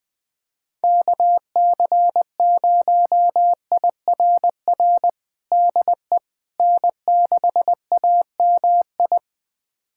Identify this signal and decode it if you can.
Morse code, heard as a keyed tone, about 21 wpm, 700 Hz. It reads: KC0IRR DE N6AMI
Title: USA Callsign DE Callsign 20wpm